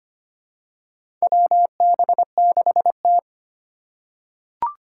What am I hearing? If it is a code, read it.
Morse code, heard as a keyed tone, about 25 wpm, 700 Hz. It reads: WB6T
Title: USA Call Signs - 3x Character Spacing - 25wpm